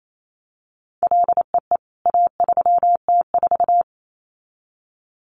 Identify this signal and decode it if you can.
Morse code, heard as a keyed tone, about 28 wpm, 700 Hz. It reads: LEE A3T4
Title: CWT Exchange 28wpm